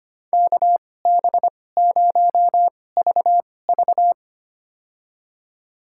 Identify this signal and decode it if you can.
Morse code, heard as a keyed tone, about 25 wpm, spaced as for 18 wpm, 700 Hz. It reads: KB0VV
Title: USA Call Signs - 5 Characters - 2x Character Spacing - 25wpm